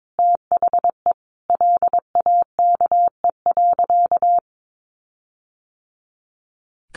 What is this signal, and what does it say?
Morse code, heard as a keyed tone, about 22 wpm, 700 Hz. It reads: THE LAKE.
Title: Sentences from Top 1300 Words 22wpm